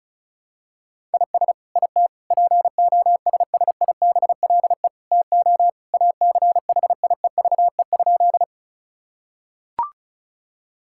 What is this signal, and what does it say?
Morse code, heard as a keyed tone, about 36 wpm, 700 Hz. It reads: IS IT POSSIBLE TO ACHIEVE?
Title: Sentences from Top 1000 Words - Review Part 6 - 35wpm